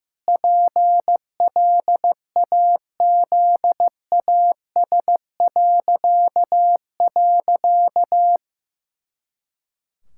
Morse code, heard as a keyed tone, about 15 wpm, 700 Hz. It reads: PLAZAS..
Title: Pangram Challenge v2 - 15wpm